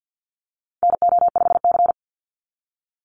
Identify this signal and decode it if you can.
Morse code, heard as a keyed tone, about 50 wpm, 700 Hz. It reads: NO5C
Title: USA Call Signs 50wpm